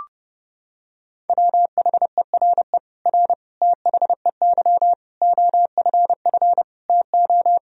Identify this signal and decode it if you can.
Morse code, heard as a keyed tone, about 30 wpm, 700 Hz. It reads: WHERE R THEY OFF TO
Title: Sentences from Top 200 Words and Abbreviations 30wpm